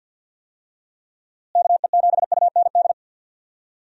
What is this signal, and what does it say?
Morse code, heard as a keyed tone, about 47 wpm, 700 Hz. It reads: KE7UND
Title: USA Call Signs - 6 Characters - Repeated 3x - 50wpm